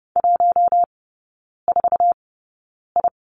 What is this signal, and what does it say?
Morse code, heard as a keyed tone, about 30 wpm, spaced as for 9 wpm, 690 Hz. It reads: 14I
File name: Lesson 17e - Letter u - Three Characters 30wpm